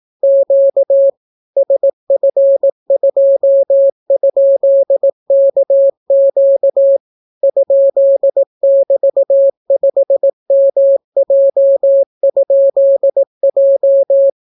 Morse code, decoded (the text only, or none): Q SF2?KQ ?<BT>5MJ?J